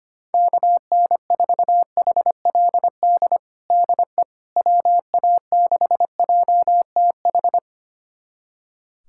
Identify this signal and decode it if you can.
Morse code, heard as a keyed tone, about 25 wpm, 700 Hz. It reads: KN4HLD DE WA6JTH
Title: USA Callsign DE Callsign 25wpm